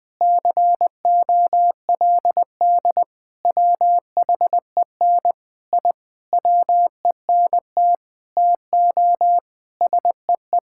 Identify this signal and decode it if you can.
Morse code, heard as a keyed tone, about 20 wpm, 700 Hz. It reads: COLD WHEN I WENT TO SEE
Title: Sentences from Top 1000 Words - Review Part 5 - 20wpm